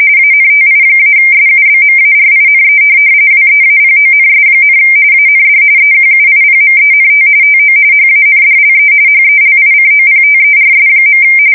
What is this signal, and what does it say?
RTTY, about 45.45 baud. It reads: ANTI-SUBMARINE WEAPONS, AND EQUIPMENT TO HANDLE HELICOPTERS.